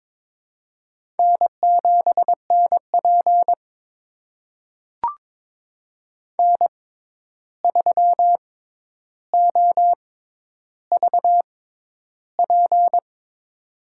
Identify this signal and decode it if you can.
Morse code, heard as a keyed tone, about 22 wpm, spaced as for 11 wpm, 700 Hz. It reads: N7NP N3OVP
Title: USA Call Signs - 6x Character Spacing - 22wpm